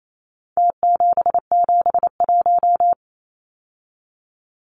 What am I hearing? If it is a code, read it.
Morse code, heard as a keyed tone, about 28 wpm, 700 Hz. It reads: T771
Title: Numbers - Contest Style - 4-Digit 28wpm